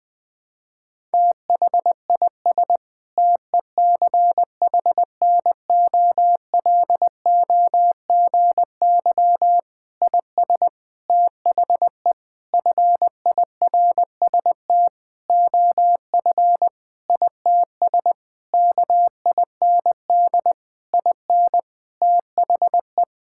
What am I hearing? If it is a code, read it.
Morse code, heard as a keyed tone, about 20 wpm, 700 Hz. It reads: THIS TECHNOLOGY IS THE FIRST OF ITS KIND IN THE
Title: Sentences with 8-Word Phrases - Spoken Beginning - 20wpm